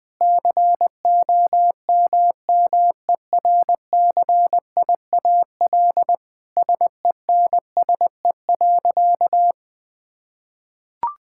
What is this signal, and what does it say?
Morse code, heard as a keyed tone, about 21 wpm, 700 Hz. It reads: COMMERCIAL SENSE.
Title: Sentences from Top 1000 Words - Review Part 5 - 20wpm